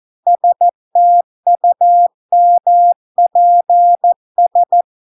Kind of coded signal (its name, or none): Morse code